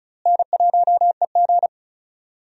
Morse code, heard as a keyed tone, about 34 wpm, 700 Hz. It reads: N1EG
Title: USA Call Signs - 4 Characters - Repeated 3x - 35wpm